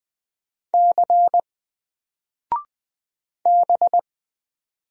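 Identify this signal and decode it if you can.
Morse code, heard as a keyed tone, about 20 wpm, 700 Hz. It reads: C B
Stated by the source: Single letter and number Morse drill at 20 wpm, from N4LCD.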